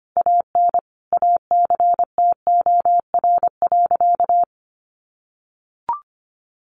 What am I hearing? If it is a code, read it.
Morse code, heard as a keyed tone, about 25 wpm, 700 Hz. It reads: AN ACTOR.
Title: Sentences from Top 1500 Words 25wpm